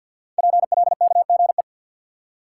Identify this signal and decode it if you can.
Morse code, heard as a keyed tone, about 48 wpm, 690 Hz. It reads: PLKGE
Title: Code Groups - 5 Letters - 50wpm